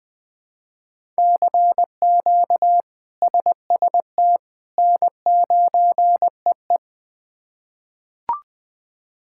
Morse code, heard as a keyed tone, about 20 wpm, 700 Hz. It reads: CQ SST N9EE
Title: K1USN SST QSO Practice - 20wpm